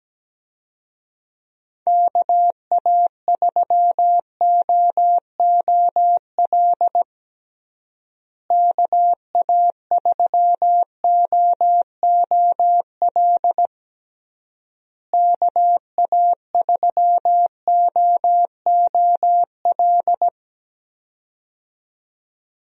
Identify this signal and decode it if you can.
Morse code, heard as a keyed tone, about 17 wpm, 700 Hz. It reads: KA3OOL KA3OOL KA3OOL
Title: USA Call Signs - 6 Characters - Repeated 3x - 17wpm